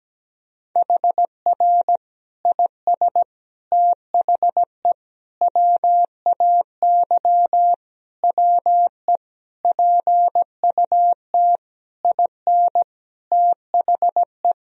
Morse code, heard as a keyed tone, about 17 wpm, 700 Hz. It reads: HR IS THE WAY WE PUT IN THE
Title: Sentences from Top 200 Words and Abbreviations 17wpm